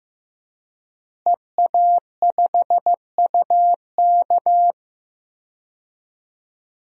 Morse code, heard as a keyed tone, about 15 wpm, 700 Hz. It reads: EA5UK
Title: DXCC-100 Call Signs - 5 Characters - Repeated 3x - 15wpm